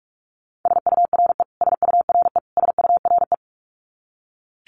Morse code, heard as a keyed tone, about 44 wpm, 700 Hz. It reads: SURE SURE SURE
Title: Top 500 Words Repeated 45wpm